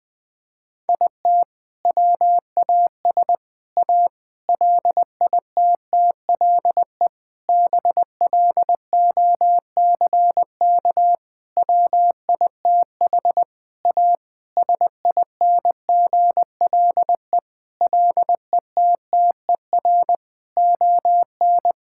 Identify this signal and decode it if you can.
Morse code, heard as a keyed tone, about 20 wpm, 700 Hz. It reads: IT WAS A LITTLE BLOCK WITH A SINGLE LETTER ON
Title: Sentences from Top 1600 Words 20wpm